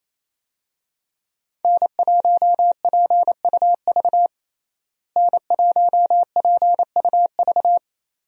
Morse code, heard as a keyed tone, about 28 wpm, 700 Hz. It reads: N1PUV N1PUV
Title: USA Call Signs - 5 Characters - Repeated 2x - 28wpm